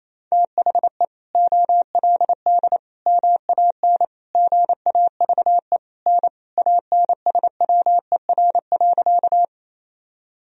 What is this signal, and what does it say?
Morse code, heard as a keyed tone, about 28 wpm, 700 Hz. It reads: THE OLD MAN GAVE N ANSWER.
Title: Sentences from Top 500 Words and Abbreviations 28wpm